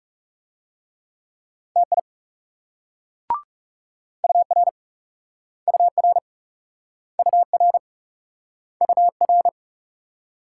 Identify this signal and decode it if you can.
Morse code, heard as a keyed tone, about 40 wpm, 700 Hz. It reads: TI UR UR UR UR